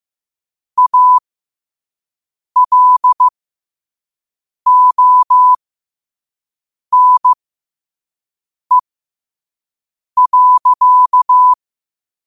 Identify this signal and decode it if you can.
Morse code, heard as a keyed tone, about 15 wpm, spaced as for 5 wpm, 1000 Hz. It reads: ALONE.
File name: qotdfiles / 5wpm